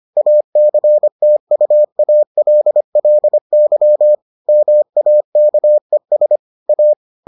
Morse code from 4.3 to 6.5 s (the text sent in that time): MAKES